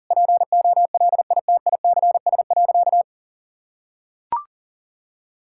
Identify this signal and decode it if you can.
Morse code, heard as a keyed tone, about 37 wpm, 700 Hz. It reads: POLITICS.
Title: Sentences from Top 1000 Words - Review Part 6 - 40wpm